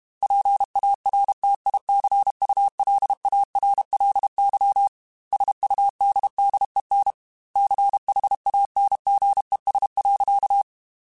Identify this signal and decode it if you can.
Morse code, heard as a keyed tone, about 32 wpm, 800 Hz. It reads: PARTICULARLY SUDDEN CHANGES.
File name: Technical Subjects / Technical Subjects-mp3s / Reactance @30